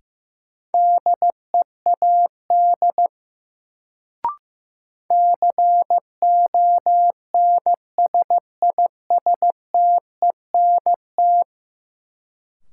Morse code, heard as a keyed tone, about 15 wpm, 700 Hz. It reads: DEAD CONSISTENT
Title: Sentences from Top 1500 Words - New Word Drill - 15wpm